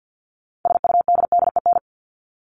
Morse code, heard as a keyed tone, about 50 wpm, 700 Hz. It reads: SUDDEN
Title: LY Suffix - Top 100 Words with Base Words 50wpm